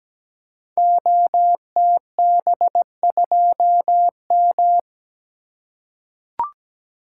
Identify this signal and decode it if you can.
Morse code, heard as a keyed tone, about 17 wpm, 700 Hz. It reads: OTB2M
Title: Code Groups - 5 Letter:Numbers - 17wpm